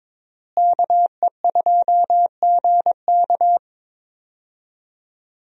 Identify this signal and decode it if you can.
Morse code, heard as a keyed tone, about 22 wpm, 700 Hz. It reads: KE2GK
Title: USA Call Signs - 5 Characters - Repeated 3x - 22wpm